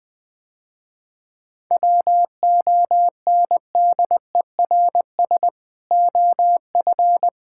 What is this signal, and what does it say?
Morse code, heard as a keyed tone, about 20 wpm, 700 Hz. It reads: WONDERS OF
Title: Sentences with 3-Word Phrases - Spoken Beginning - 20wpm